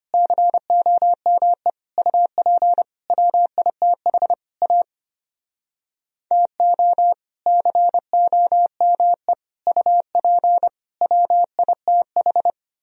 Morse code, heard as a keyed tone, about 32 wpm, 700 Hz. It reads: COME UP WITH A TO COME UP WITH